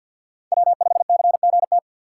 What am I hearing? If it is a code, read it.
Morse code, heard as a keyed tone, about 48 wpm, 690 Hz. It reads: W5CGT
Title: USA Call Signs - 5 Characters - 6x Character Spacing - 50wpm